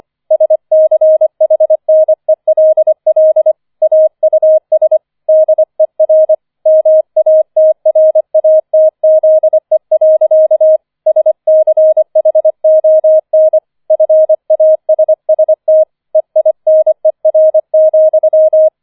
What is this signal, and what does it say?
Morse code, heard as a keyed tone, about 25 wpm, 630 Hz. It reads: SCHNELL AUS DER MATRATZE. SCHON FASST EINER,
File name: 005 Fünfter Streich 25wpm